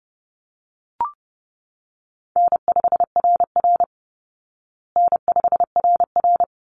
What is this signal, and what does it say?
Morse code, heard as a keyed tone, about 30 wpm, 700 Hz. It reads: N5RR N5RR
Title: Easy Call Signs Repeated 2x - 30wpm